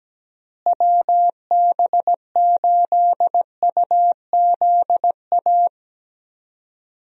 Morse code, heard as a keyed tone, about 17 wpm, 700 Hz. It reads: WB8UZA
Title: USA Call Signs - 6 Characters - Repeated 2x - 17wpm